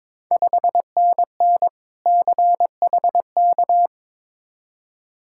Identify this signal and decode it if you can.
Morse code, heard as a keyed tone, about 22 wpm, 700 Hz. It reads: 5NN CHK
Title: Ontario QSO Party Exchange 22wpm